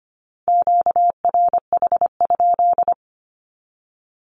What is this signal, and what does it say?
Morse code, heard as a keyed tone, about 25 wpm, 700 Hz. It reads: QRH?
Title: Q-Codes 25wpm